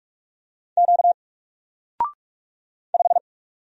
Morse code, heard as a keyed tone, about 44 wpm, 700 Hz. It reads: Q 5
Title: Single Character (Letter:Number:Punctuation:Prosign) Alt 45wpm